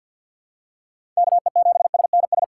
Morse code, heard as a keyed tone, about 48 wpm, 700 Hz. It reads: KE7SNS
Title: USA Call Signs - 6 Characters - 4x Character Spacing - 50wpm